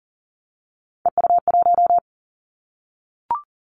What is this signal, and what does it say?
Morse code, heard as a keyed tone, about 41 wpm, 700 Hz. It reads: EU1
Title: Callsign Prefixes for Top 100 DXCC - 40wpm